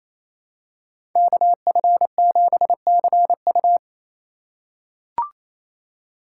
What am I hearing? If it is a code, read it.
Morse code, heard as a keyed tone, about 28 wpm, 700 Hz. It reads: KF7CU
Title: USA Call Signs - 5 Characters - 4x Character Spacing - 28wpm